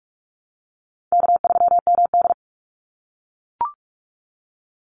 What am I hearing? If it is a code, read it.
Morse code, heard as a keyed tone, about 44 wpm, 700 Hz. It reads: K3MD
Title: Call Signs - 45wpm